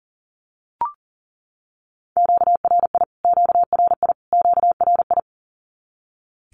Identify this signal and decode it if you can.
Morse code, heard as a keyed tone, about 40 wpm, 700 Hz. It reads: QRI QRI QRI
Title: Q-Codes Repeated 3x 40wpm